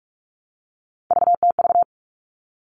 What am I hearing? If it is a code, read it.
Morse code, heard as a keyed tone, about 44 wpm, 700 Hz. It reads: VTV
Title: All Cut Numbers - 3-Digit 45wpm